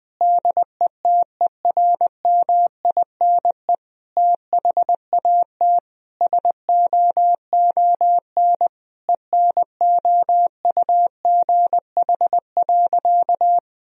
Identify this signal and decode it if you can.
Morse code, heard as a keyed tone, about 20 wpm, 700 Hz. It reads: DETERMINE THAT SOON ENOUGH.
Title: Sentences from Top 1000 Words - Review Part 4 - 20wpm